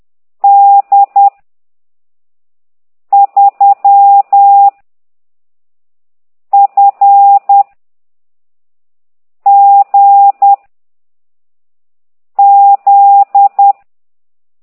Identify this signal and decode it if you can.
Morse code, heard as a keyed tone, about 10 wpm, 800 Hz. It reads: D 3 F G Z
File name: ct2idw / exerc15